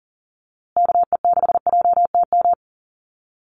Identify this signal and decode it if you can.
Morse code, heard as a keyed tone, about 40 wpm, 700 Hz. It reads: KE6JTM
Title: USA Call Signs Repeated 3x 40wpm